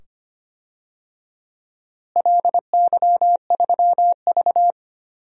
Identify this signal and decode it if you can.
Morse code, heard as a keyed tone, about 25 wpm, 700 Hz. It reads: LY3V
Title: DXCC-100 Call Signs - 4 Characters - Repeated 2x - 25wpm